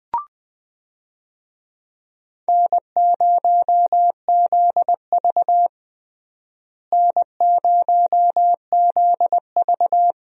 Morse code, heard as a keyed tone, about 21 wpm, 700 Hz. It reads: N0ZV N0ZV
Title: USA Call Signs - 4 Characters - Repeated 2x - 20wpm